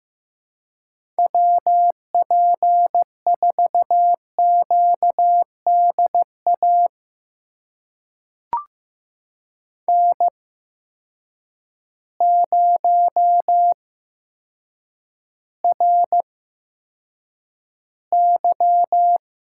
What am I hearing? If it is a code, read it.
Morse code, heard as a keyed tone, about 15 wpm, spaced as for 8 wpm, 700 Hz. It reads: WP4QDA N0RY
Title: USA Call Signs - 8x Character Spacing - 15wpm